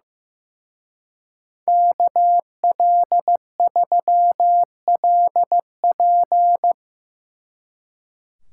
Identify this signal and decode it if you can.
Morse code, heard as a keyed tone, about 17 wpm, 700 Hz. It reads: KL3LP
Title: Speed Racing - USA 2x2 Call Signs 15wpm